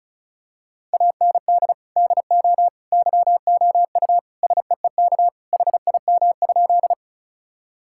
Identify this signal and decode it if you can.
Morse code, heard as a keyed tone, about 36 wpm, 700 Hz. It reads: AND DO YOU SEEK HIM?
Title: Sentences from Top 1600 Words 35wpm